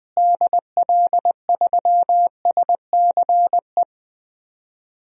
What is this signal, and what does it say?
Morse code, heard as a keyed tone, about 20 wpm, 700 Hz. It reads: DL3SCE
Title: DXCC-100 Call Signs - 6 Characters - Repeated 3x - 20wpm